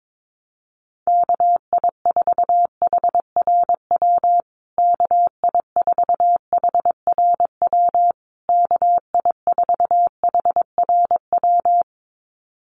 Morse code, heard as a keyed tone, about 22 wpm, 700 Hz. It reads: KI4HRW KI4HRW KI4HRW
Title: USA Call Signs Repeated 3x 22wpm